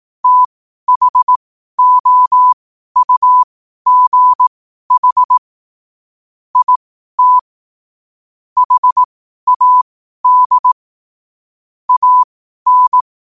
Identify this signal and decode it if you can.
Morse code, heard as a keyed tone, about 18 wpm, spaced as for 12 wpm, 1000 Hz. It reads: THOUGH IT HAD AN
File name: qotdfiles / 12wpm